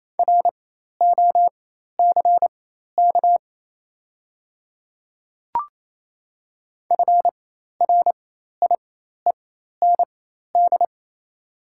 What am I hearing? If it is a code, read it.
Morse code, heard as a keyed tone, about 28 wpm, spaced as for 13 wpm, 690 Hz. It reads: ROCK FRIEND
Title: 4x Character Spacing - Top 500 Words - 28wpm